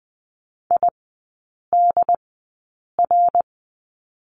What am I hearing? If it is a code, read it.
Morse code, heard as a keyed tone, about 20 wpm, spaced as for 8 wpm, 700 Hz. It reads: IDR
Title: 067 Lesson 12e - Letter d - Three Characters 20wpm